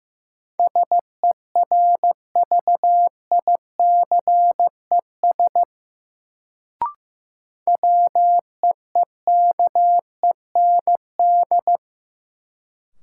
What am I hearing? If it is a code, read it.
Morse code, heard as a keyed tone, about 15 wpm, 700 Hz. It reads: SERVICES WEEKEND
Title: Sets of 5 Words - New Word Drill - 15wpm